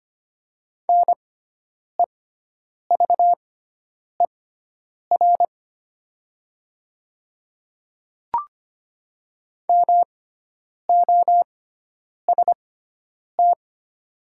Morse code, heard as a keyed tone, about 25 wpm, spaced as for 9 wpm, 700 Hz. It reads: NEVER MOST
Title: 6x Character Spacing - Top 300 Words - 25wpm